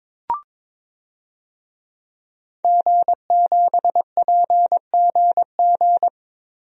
Morse code, heard as a keyed tone, about 22 wpm, 700 Hz. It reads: G7PGG
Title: DXCC-100 Call Signs - 5 Characters - 22wpm